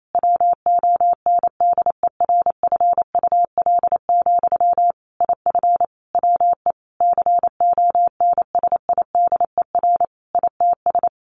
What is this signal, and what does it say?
Morse code, heard as a keyed tone, about 28 wpm, 700 Hz. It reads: WONDERFUL, IF WE CONSIDER ITS
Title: Sentences from Top 1100 Words 28wpm